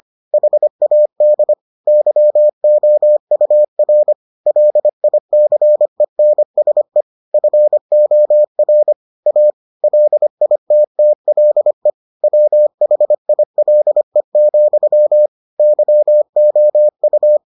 Morse code, heard as a keyed tone, about 25 wpm, 600 Hz. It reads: HAD YOUR LICENSE FOR A LITTLE WHILE, YOU